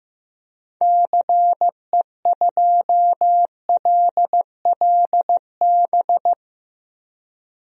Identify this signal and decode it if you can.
Morse code, heard as a keyed tone, about 15 wpm, 700 Hz. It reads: CE2LLB
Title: DXCC-100 Call Signs - 6 Characters - 15wpm